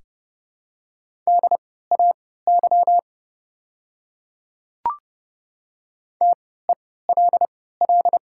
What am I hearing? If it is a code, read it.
Morse code, heard as a keyed tone, about 30 wpm, spaced as for 17 wpm, 700 Hz. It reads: DAY TELL
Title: 3x Character Spacing - Top 200 Words - 30wpm